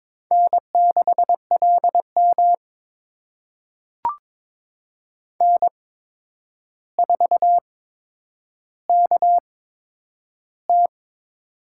Morse code, heard as a keyed tone, about 22 wpm, spaced as for 9 wpm, 700 Hz. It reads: N6LM N4KT
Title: USA Call Signs - 4 Characters - 8x Character Spacing - 22wpm